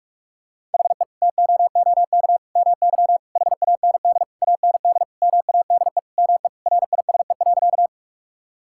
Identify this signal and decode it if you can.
Morse code, heard as a keyed tone, about 46 wpm, 700 Hz. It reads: HE TOOK MY HAND AND MADE ME RISE.
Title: Sentences from Top 1000 Words - Review Part 6 - 45wpm